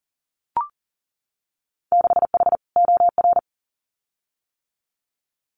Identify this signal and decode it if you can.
Morse code, heard as a keyed tone, about 38 wpm, 700 Hz. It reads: 6H OR
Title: Winter Field Day Exchanges Encore 40wpm